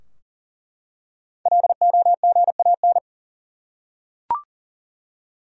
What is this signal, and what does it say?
Morse code, heard as a keyed tone, about 40 wpm, 700 Hz. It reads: LOGAN
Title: USA Common Names - Max 5 Letters - 40wpm